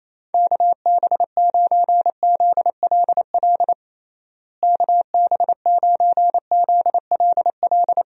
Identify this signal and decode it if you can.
Morse code, heard as a keyed tone, about 28 wpm, 700 Hz. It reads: KB9ZLL KB9ZLL
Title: USA Call Signs - 6 Characters - Repeated 3x - 28wpm